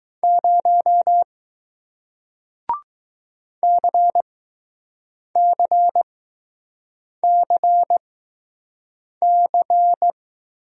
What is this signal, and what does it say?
Morse code, heard as a keyed tone, about 20 wpm, spaced as for 6 wpm, 700 Hz. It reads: 0 CCCC